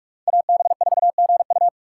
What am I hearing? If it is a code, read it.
Morse code, heard as a keyed tone, about 44 wpm, 700 Hz. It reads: AB4GU
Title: USA Call Signs - 5 Characters - 3x Character Spacing - 45wpm